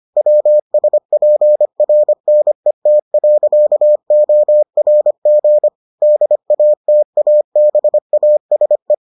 Morse code, heard as a keyed tone, about 25 wpm, 600 Hz. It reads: WSPRNET.ORG DATABASE